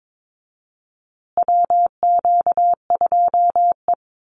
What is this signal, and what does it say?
Morse code, heard as a keyed tone, about 22 wpm, 700 Hz. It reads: WQ2E
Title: Call Signs Repeated 2x - 22wpm